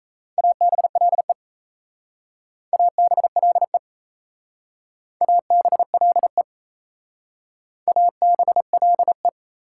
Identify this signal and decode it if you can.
Morse code, heard as a keyed tone, about 34 wpm, 700 Hz. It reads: ABLE ABLE ABLE ABLE
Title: Speed-Racing Prefixes and Suffixes 28wpm